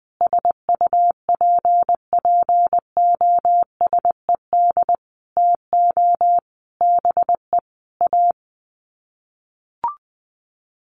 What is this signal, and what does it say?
Morse code, heard as a keyed tone, about 20 wpm, 700 Hz. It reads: SUPPOSED TO BE A
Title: Sets of 5 Words Encore 20wpm